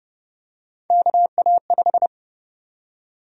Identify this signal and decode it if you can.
Morse code, heard as a keyed tone, about 31 wpm, 700 Hz. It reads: KA5
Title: USA Call Sign Prefixes - 30wpm